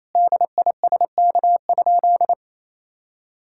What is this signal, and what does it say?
Morse code, heard as a keyed tone, about 27 wpm, 690 Hz. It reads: DISK?
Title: Sentences from Top 1000 Words - Review Part 6 - 28wpm